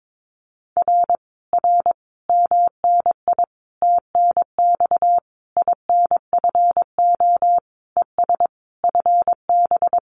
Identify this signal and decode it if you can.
Morse code, heard as a keyed tone, about 22 wpm, 700 Hz. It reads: R R MNI TNX INFO ES FB
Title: Common QSO Phrases Encore 22wpm